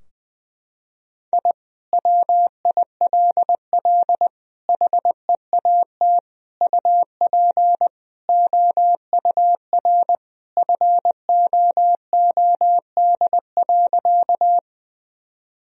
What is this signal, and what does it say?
Morse code, heard as a keyed tone, about 21 wpm, 700 Hz. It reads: I WILL HEAT UP OUR FOOD.
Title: Sentences from Top 1000 Words - Review Part 3 - 20wpm